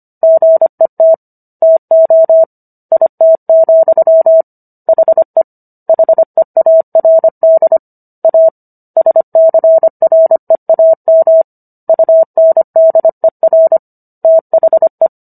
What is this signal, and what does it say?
Morse code, heard as a keyed tone, about 25 wpm, 650 Hz. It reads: GET TO IT, HE HEARD A SCREAM UNDER THE